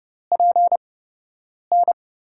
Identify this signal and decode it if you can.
Morse code, heard as a keyed tone, about 31 wpm, spaced as for 8 wpm, 700 Hz. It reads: PN
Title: USA Call Signs - 5 Characters - 8x Character Spacing - 30wpm